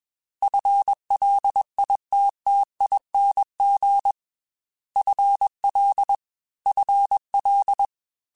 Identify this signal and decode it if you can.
Morse code, heard as a keyed tone, about 21 wpm, 800 Hz. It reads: FLITTING FL FL
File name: FL / FL mp3s / FL Prefixes 3x2 @20